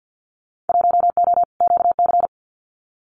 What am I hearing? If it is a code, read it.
Morse code, heard as a keyed tone, about 50 wpm, 700 Hz. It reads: 1O QC
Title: Winter Field Day Exchanges 50wpm